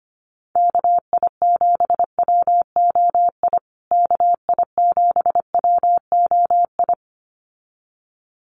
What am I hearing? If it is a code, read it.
Morse code, heard as a keyed tone, about 25 wpm, 700 Hz. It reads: KI7WOI KI7WOI
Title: USA Call Signs Repeated 2x 25wpm